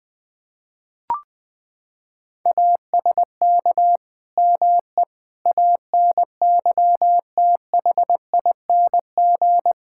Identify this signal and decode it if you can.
Morse code, heard as a keyed tone, about 21 wpm, 700 Hz. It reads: ASK ME ANYTHING
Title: Sentences from Top 1600 Words 20wpm